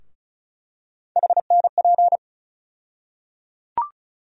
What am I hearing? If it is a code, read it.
Morse code, heard as a keyed tone, about 34 wpm, 700 Hz. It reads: HNP
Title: USA Call Sign Postfixes - 3 Letters 35wpm